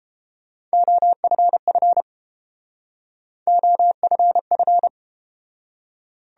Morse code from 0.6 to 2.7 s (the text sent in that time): OFF